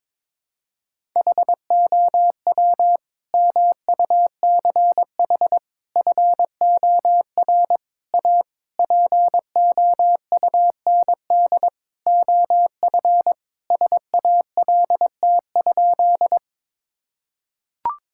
Morse code, heard as a keyed tone, about 22 wpm, 700 Hz. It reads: HOW MUCH FOR A POUND OF SALT?
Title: Sentences from Top 1000 Words - Review Part 5 - 22wpm